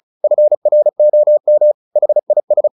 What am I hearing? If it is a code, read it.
Morse code, heard as a keyed tone, about 35 wpm, 600 Hz. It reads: FROM HIS
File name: the old man and the sea / the old man and the sea 1 35wpm